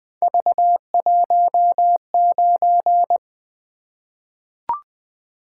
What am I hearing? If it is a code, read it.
Morse code, heard as a keyed tone, about 20 wpm, 700 Hz. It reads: V19
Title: Three Letter-Number Combinations - 20wpm